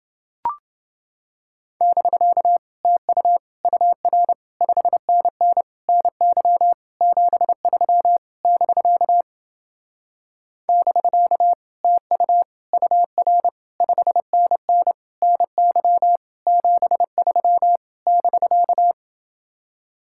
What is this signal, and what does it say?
Morse code, heard as a keyed tone, about 29 wpm, 700 Hz. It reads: <BK> TU UR 5NN NY 73 <BK> <BK> TU UR 5NN NY 73 <BK>